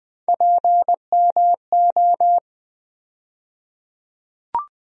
Morse code, heard as a keyed tone, about 20 wpm, 700 Hz. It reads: PMO
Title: USA Call Sign Postfixes - 3 Letters 20wpm